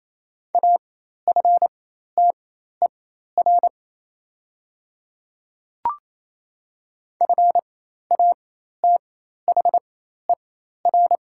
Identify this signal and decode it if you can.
Morse code, heard as a keyed tone, about 28 wpm, spaced as for 13 wpm, 700 Hz. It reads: AFTER FATHER
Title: 4x Character Spacing - Top 200 Words - 28wpm